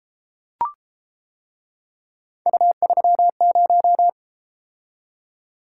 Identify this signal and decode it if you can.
Morse code, heard as a keyed tone, about 31 wpm, 700 Hz. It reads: U30